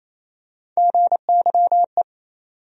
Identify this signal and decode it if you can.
Morse code, heard as a keyed tone, about 28 wpm, 700 Hz. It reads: GYE
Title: USA Call Sign Postfixes - 3 Letters 28wpm